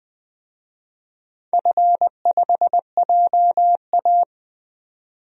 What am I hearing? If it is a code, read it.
Morse code, heard as a keyed tone, about 20 wpm, 700 Hz. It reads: F5JA
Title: DXCC-100 Call Signs - 4 Characters - 20wpm